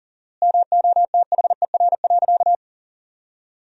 Morse code, heard as a keyed tone, about 40 wpm, 700 Hz. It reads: MOTHER.
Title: 4x Word-Spacing - Sentences from Top 400 Words - 40wpm